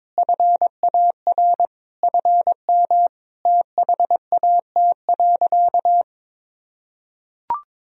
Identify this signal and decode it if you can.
Morse code, heard as a keyed tone, about 22 wpm, 700 Hz. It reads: FAR FM THAT.
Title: Sentences from Top 300 Words and Abbreviations 22wpm